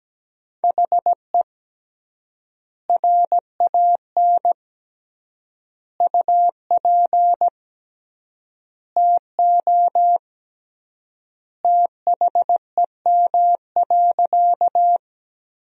Morse code, heard as a keyed tone, about 17 wpm, 700 Hz. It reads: HE RAN UP TO THEM.
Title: 3x Word-Spacing - Sentences from Top 500 Words - 17wpm